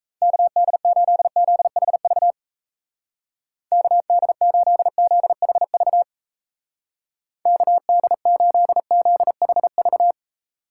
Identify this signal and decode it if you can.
Morse code, heard as a keyed tone, about 34 wpm, 700 Hz. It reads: KD8ZHV KD8ZHV KD8ZHV
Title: Speed Racing - USA 2x3 Call Signs 28wpm